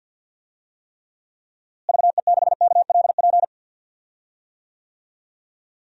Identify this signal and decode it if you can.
Morse code, heard as a keyed tone, about 51 wpm, 700 Hz. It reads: VE6KLP
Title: DXCC-100 Call Signs - 6 Characters - Repeated 2x - 50wpm